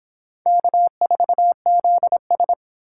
Morse code, heard as a keyed tone, about 29 wpm, 700 Hz. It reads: K4ZS